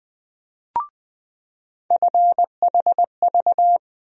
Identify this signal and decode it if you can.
Morse code, heard as a keyed tone, about 20 wpm, 700 Hz. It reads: FHV
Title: USA Call Sign Postfixes - 3 Letters 20wpm